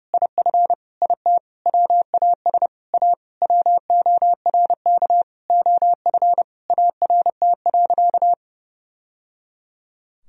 Morse code, heard as a keyed tone, about 32 wpm, 700 Hz. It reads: IF IT WAS A WORK OF ART.
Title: Sentences from Top 1000 Words - Review Part 3 - 30wpm